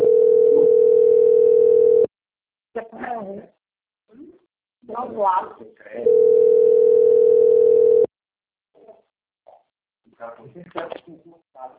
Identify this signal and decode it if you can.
Ringback tone (North America), 440 + 480 Hz, with speech.
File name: rg-1600-unknown- - - .8825